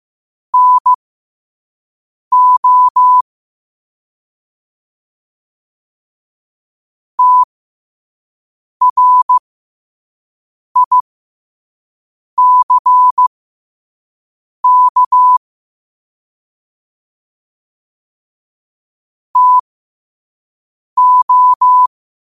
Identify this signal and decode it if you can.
Morse code, heard as a keyed tone, about 15 wpm, spaced as for 5 wpm, 1000 Hz. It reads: NO TRICK TO
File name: qotdfiles / 5wpm